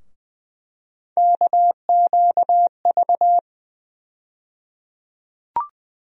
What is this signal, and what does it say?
Morse code, heard as a keyed tone, about 20 wpm, 700 Hz. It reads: KQV
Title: USA Call Sign Postfixes - 3 Letters 20wpm